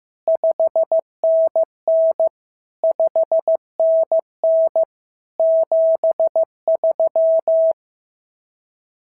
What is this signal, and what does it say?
Morse code, heard as a keyed tone, about 15 wpm, 650 Hz. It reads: 5NN 5NN 73
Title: SOTA QSO Practice 15wpm